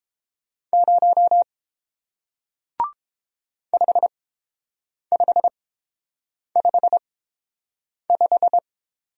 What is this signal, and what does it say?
Morse code, heard as a keyed tone, about 29 wpm, spaced as for 7 wpm, 700 Hz. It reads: 0 5555